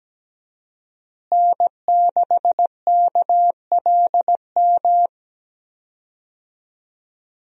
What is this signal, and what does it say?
Morse code, heard as a keyed tone, about 17 wpm, 700 Hz. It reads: N6KLM
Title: USA Call Signs - 5 Characters - Repeated 2x - 17wpm